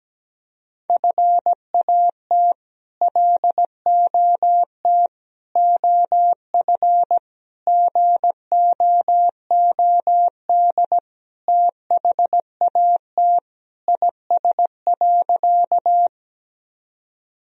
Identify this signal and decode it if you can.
Morse code, heard as a keyed tone, about 17 wpm, 700 Hz. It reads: FAT LOT OF GOOD THAT IS.
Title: Sentences from Top 1000 Words - Review Part 4 - 17wpm